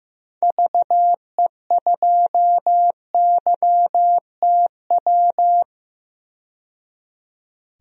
Morse code, heard as a keyed tone, about 15 wpm, 700 Hz. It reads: VE2YTW
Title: DXCC-100 Call Signs - 6 Characters - 15wpm